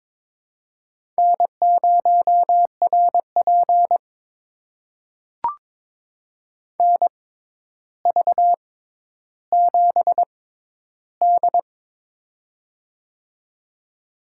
Morse code, heard as a keyed tone, about 22 wpm, spaced as for 11 wpm, 700 Hz. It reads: N0RP NV7D
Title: USA Call Signs - 4 Characters - 6x Character Spacing - 22wpm